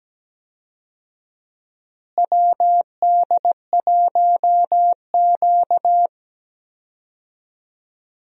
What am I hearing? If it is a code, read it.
Morse code, heard as a keyed tone, about 17 wpm, 700 Hz. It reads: WD1Q
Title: USA Call Signs - 4 Characters - Repeated 2x - 17wpm